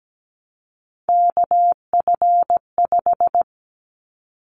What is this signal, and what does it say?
Morse code, heard as a keyed tone, about 17 wpm, 700 Hz. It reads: KF5
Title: Callsign Prefixes for Top 25 DXCC - Repeated 3x - 17wpm